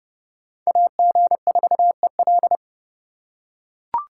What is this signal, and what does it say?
Morse code, heard as a keyed tone, about 31 wpm, 700 Hz. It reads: AG4EL